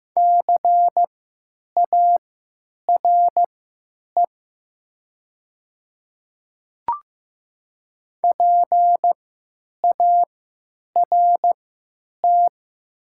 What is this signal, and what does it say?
Morse code, heard as a keyed tone, about 15 wpm, spaced as for 9 wpm, 700 Hz. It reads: CARE PART
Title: 3x Character Spacing - Top 300 Words - 15wpm